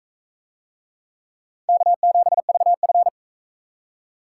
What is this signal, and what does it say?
Morse code, heard as a keyed tone, about 38 wpm, 700 Hz. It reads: K7VF